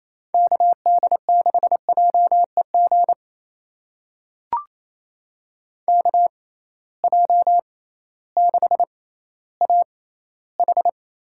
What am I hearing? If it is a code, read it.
Morse code, heard as a keyed tone, about 28 wpm, spaced as for 15 wpm, 700 Hz. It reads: KD6JEG KJ6AH
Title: USA Call Signs - 6 Characters - 6x Character Spacing - 28wpm